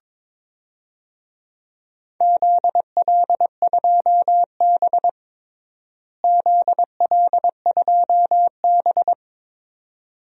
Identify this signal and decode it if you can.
Morse code, heard as a keyed tone, about 21 wpm, 700 Hz. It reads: ZL2B ZL2B
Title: DXCC-100 Call Signs - 4 Characters - Repeated 2x - 22wpm